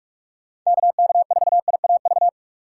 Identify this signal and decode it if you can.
Morse code, heard as a keyed tone, about 48 wpm, 700 Hz. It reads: KK4IAV
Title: USA Call Signs - 6 Characters - 6x Character Spacing - 45wpm